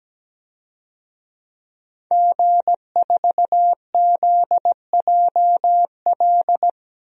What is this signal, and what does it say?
Morse code, heard as a keyed tone, about 17 wpm, 700 Hz. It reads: G4ZJL
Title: DXCC-100 Call Signs - 5 Characters - 17wpm